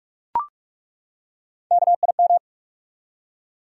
Morse code, heard as a keyed tone, about 42 wpm, 700 Hz. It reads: KIM
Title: USA Common Names - Max 5 Letters - 45wpm